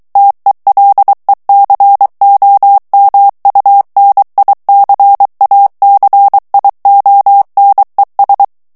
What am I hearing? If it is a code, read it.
Morse code, heard as a keyed tone, about 23 wpm, 800 Hz. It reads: TELECOMUNICACIONES
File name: morse mp3 / ESPAGNOL TEXTE EN CLAIR 2 VITESSE 22 MOTS MINUTES